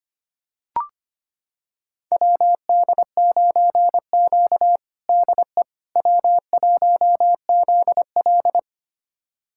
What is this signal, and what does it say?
Morse code, heard as a keyed tone, about 25 wpm, 700 Hz. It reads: WD9Q DE W1ZL
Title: USA Callsign DE Callsign - 4 Characters - 25wpm